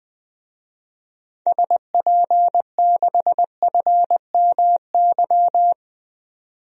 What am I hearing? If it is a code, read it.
Morse code, heard as a keyed tone, about 20 wpm, 700 Hz. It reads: SP6FMY
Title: DXCC-100 Call Signs - 6 Characters - 20wpm